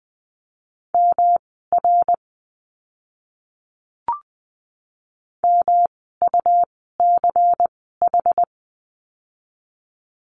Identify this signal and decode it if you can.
Morse code, heard as a keyed tone, about 19 wpm, spaced as for 14 wpm, 700 Hz. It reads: MR MUCH
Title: 244 Lesson 39f - Letter z - Words 20wpm